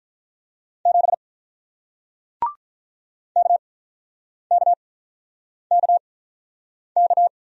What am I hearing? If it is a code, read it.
Morse code, heard as a keyed tone, about 47 wpm, spaced as for 9 wpm, 700 Hz. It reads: 7 KKKK